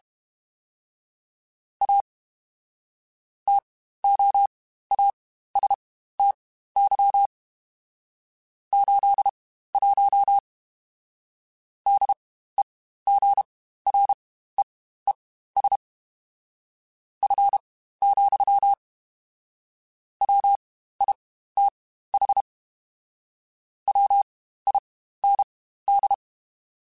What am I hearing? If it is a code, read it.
Morse code, heard as a keyed tone, about 32 wpm, spaced as for 15 wpm, 800 Hz. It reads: A TOASTY 81 DEGREES F, WITH WIND
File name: Spaced Characters / Spaced Characters-mp3s / Weather 5 12 sp @30